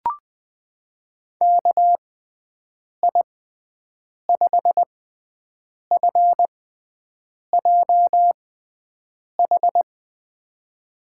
Morse code, heard as a keyed tone, about 20 wpm, spaced as for 7 wpm, 700 Hz. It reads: KI5FJH
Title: USA Call Signs - 6 Characters - 6x Character Spacing - 20wpm